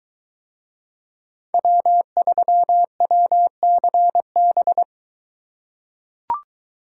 Morse code, heard as a keyed tone, about 21 wpm, 700 Hz. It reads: W3WCB